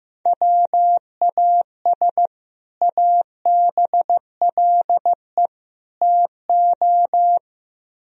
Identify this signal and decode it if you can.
Morse code, heard as a keyed tone, about 15 wpm, 700 Hz. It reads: WAS ABLE TO
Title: Speed-Racing - Sets of 3 Words 15wpm